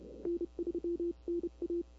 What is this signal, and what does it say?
Morse code, heard as a keyed tone, about 30 wpm, 340 Hz. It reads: N3NA
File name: CW / 2005-k5tr arrl cw dx -21:00